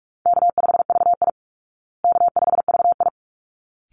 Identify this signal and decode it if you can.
Morse code, heard as a keyed tone, about 44 wpm, 700 Hz. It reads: K5VI K5VI
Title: Easy Call Signs Repeated 2x - 45wpm